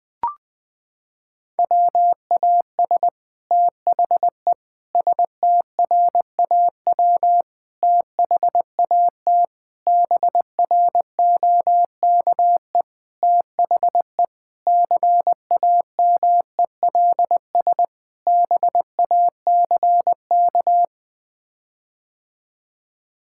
Morse code, heard as a keyed tone, about 20 wpm, 700 Hz. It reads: WAS THE STRAW THAT BROKE THE CAMELS BACK
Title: Sets of 8 Words - 20wpm